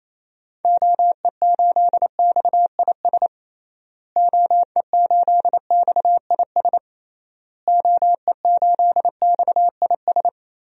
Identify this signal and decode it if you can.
Morse code, heard as a keyed tone, about 28 wpm, 700 Hz. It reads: OE8XIS OE8XIS OE8XIS
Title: DXCC-100 Call Signs - 6 Characters - Repeated 3x - 28wpm